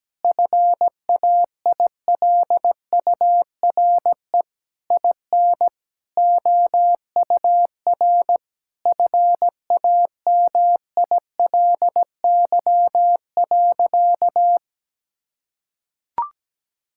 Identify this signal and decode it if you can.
Morse code, heard as a keyed tone, about 17 wpm, 700 Hz. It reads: FAILURE IN OUR FAMILY.
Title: Sentences from Top 1000 Words - Review Part 6 - 17wpm